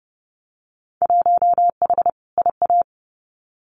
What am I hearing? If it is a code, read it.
Morse code, heard as a keyed tone, about 31 wpm, 700 Hz. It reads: 1H IA
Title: Winter Field Day Exchanges 30wpm